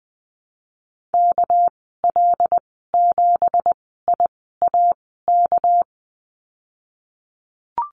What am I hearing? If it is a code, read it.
Morse code, heard as a keyed tone, about 20 wpm, spaced as for 14 wpm, 700 Hz. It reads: KL7IAK
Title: 200 Lesson 32g - Letter k - Call Signs 20wpm